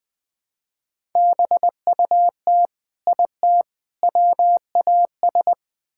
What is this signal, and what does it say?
Morse code, heard as a keyed tone, about 20 wpm, 700 Hz. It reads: BUT IT WAS
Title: Build A Phrase - 3-Word Phrases - 20wpm